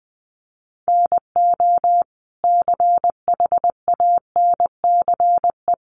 Morse code, heard as a keyed tone, about 19 wpm, 700 Hz. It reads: NO CHANCE
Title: Sentences from Top 800 Words 20wpm